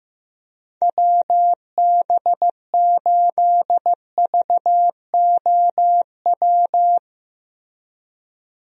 Morse code, heard as a keyed tone, about 15 wpm, 700 Hz. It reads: WB8VOW
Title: USA Call Signs - 6 Characters - 15wpm